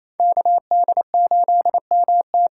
Morse code, heard as a keyed tone, about 28 wpm, 700 Hz. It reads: KD8MT
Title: USA Call Signs - 6 Characters - 8x Character Spacing - 28wpm